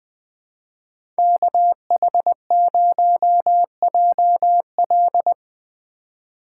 Morse code, heard as a keyed tone, about 20 wpm, 700 Hz. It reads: KH0JL
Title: USA Call Signs - 4x Character Spacing - 20wpm